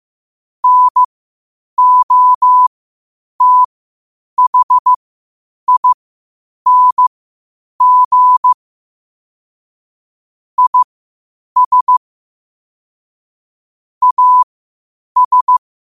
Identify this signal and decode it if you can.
Morse code, heard as a keyed tone, about 15 wpm, spaced as for 8 wpm, 1000 Hz. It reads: NOTHING IS AS
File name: qotdfiles / 8wpm